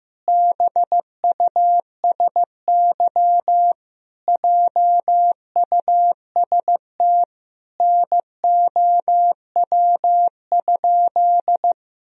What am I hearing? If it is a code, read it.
Morse code, heard as a keyed tone, about 16 wpm, 700 Hz. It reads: BUSY JUST NOW?
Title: Sentences from Top 1000 Words - Review Part 3 - 15wpm